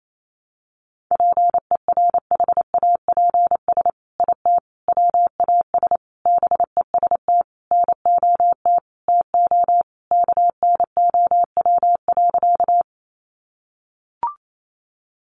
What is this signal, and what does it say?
Morse code, heard as a keyed tone, about 28 wpm, 700 Hz. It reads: PERHAPS IT WAS BEST NOT TO KNOW.
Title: Sentences from Top 600 Words 28wpm